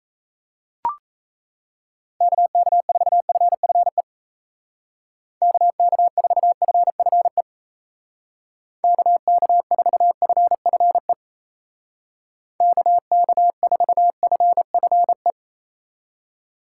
Morse code, heard as a keyed tone, about 37 wpm, 700 Hz. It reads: KK4FFE KK4FFE KK4FFE KK4FFE